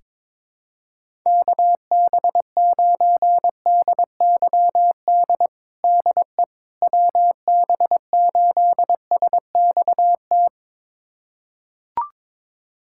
Morse code, heard as a keyed tone, about 22 wpm, 700 Hz. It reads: KB9DYD DE WB8SXT
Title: USA Callsign DE Callsign - 6 Characters - 22wpm